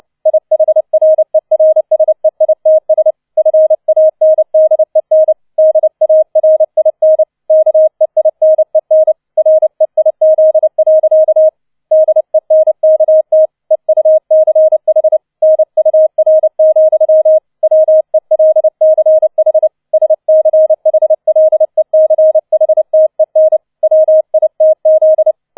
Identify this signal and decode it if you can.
Morse code, heard as a keyed tone, about 29 wpm, 630 Hz. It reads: IHRERSEITS FANDEN DARIN KEINEN REIZ. DENKT EUCH NUR, WELCH SCHLECHTEN WITZ